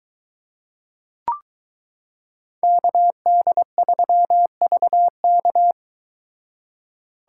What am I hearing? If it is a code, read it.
Morse code, heard as a keyed tone, about 22 wpm, 700 Hz. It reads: KD3VK